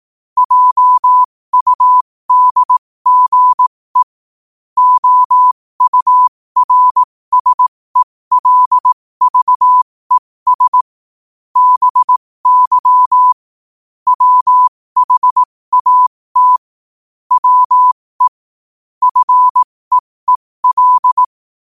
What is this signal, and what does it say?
Morse code, heard as a keyed tone, about 18 wpm, spaced as for 15 wpm, 1000 Hz. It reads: JUDGE OURSELVES BY WHAT WE FEEL
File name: qotdfiles / 15wpm